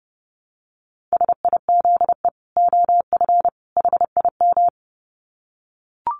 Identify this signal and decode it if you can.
Morse code, heard as a keyed tone, about 30 wpm, 700 Hz. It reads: SIZE OF HIM
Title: Dit Challenge 30wpm